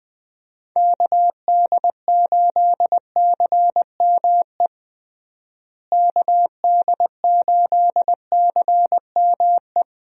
Morse code, heard as a keyed tone, about 20 wpm, 700 Hz. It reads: KD8CME KD8CME
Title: USA Call Signs - 6 Characters - Repeated 3x - 20wpm